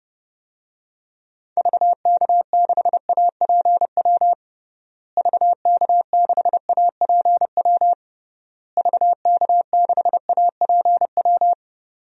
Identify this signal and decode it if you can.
Morse code, heard as a keyed tone, about 30 wpm, 700 Hz. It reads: VK6APW VK6APW VK6APW
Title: DXCC-100 Call Signs - 6 Characters - Repeated 3x - 30wpm